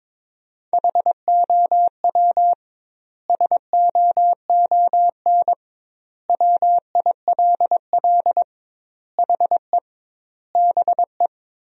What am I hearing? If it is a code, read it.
Morse code, heard as a keyed tone, about 22 wpm, 700 Hz. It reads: HOW SOON WILL HE BE
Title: 2x Word-Spacing - Sentences from Top 400 Words - 22wpm